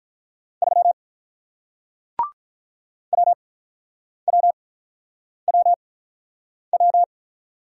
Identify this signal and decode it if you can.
Morse code, heard as a keyed tone, about 47 wpm, spaced as for 9 wpm, 700 Hz. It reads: 3 WWWW